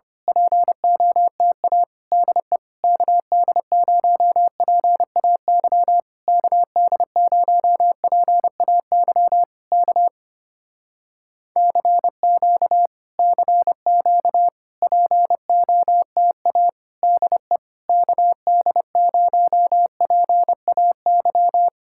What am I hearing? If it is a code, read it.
Morse code, heard as a keyed tone, about 26 wpm, 700 Hz. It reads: POTA DE KD0PAY KD0PAY K CQ CQ POTA DE KD0PAY
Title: Speed-Racing Common QSO Phrases - 25wpm